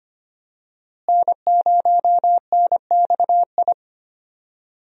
Morse code, heard as a keyed tone, about 25 wpm, 700 Hz. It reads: N0NXI
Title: USA Call Signs - 5 Characters - Repeated 3x - 25wpm